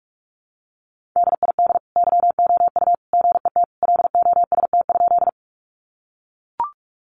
Morse code, heard as a keyed tone, about 44 wpm, 700 Hz. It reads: DID YOU GET LOST?
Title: Sentences from Top 1400 Words 45wpm